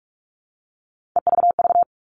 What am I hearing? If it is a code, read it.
Morse code, heard as a keyed tone, about 44 wpm, 700 Hz. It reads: EVV
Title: All Cut Numbers - 3-Digit 45wpm